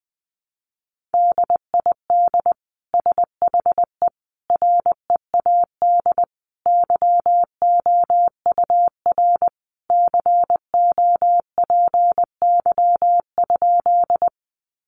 Morse code, heard as a keyed tone, about 20 wpm, 700 Hz. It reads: DID SHE READ YOUR COPY?
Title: Sentences from Top 1400 Words 20wpm